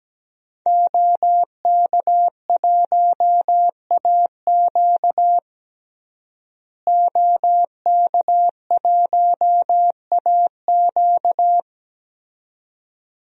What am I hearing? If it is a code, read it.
Morse code, heard as a keyed tone, about 17 wpm, 700 Hz. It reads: OK1AQ OK1AQ
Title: DXCC-100 Call Signs - 5 Characters - Repeated 2x - 17wpm